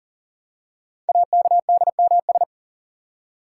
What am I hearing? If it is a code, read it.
Morse code, heard as a keyed tone, about 43 wpm, 700 Hz. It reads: AKDMS
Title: Code Groups - 5 Letters - 40wpm